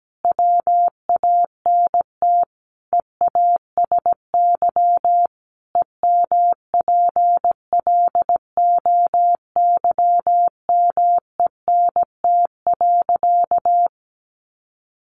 Morse code, heard as a keyed tone, about 17 wpm, 700 Hz. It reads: WANT EASY EMPLOYMENT.
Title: Sentences from Top 1500 Words 17wpm